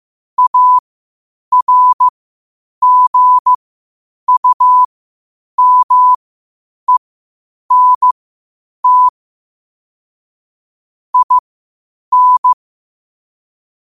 Morse code, heard as a keyed tone, about 15 wpm, spaced as for 8 wpm, 1000 Hz. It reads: ARGUMENT IN
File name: 8wpm